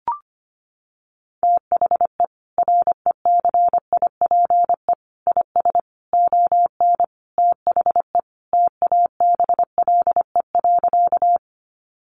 Morse code, heard as a keyed tone, about 25 wpm, 700 Hz. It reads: THE RECIPE IS ON THE TABLE.
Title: Sentences from Top 1200 Words 25wpm